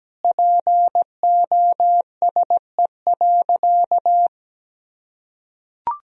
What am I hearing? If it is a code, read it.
Morse code, heard as a keyed tone, about 17 wpm, 700 Hz. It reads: POSE.
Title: Sentences from Top 1000 Words - Review Part 2 - 17wpm